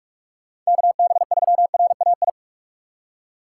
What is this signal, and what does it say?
Morse code, heard as a keyed tone, about 48 wpm, 700 Hz. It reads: KB3RAI